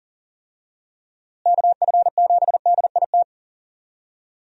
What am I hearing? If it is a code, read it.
Morse code, heard as a keyed tone, about 37 wpm, 700 Hz. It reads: KF7DIT
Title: USA Call Signs - 6 Characters - Repeated 3x - 40wpm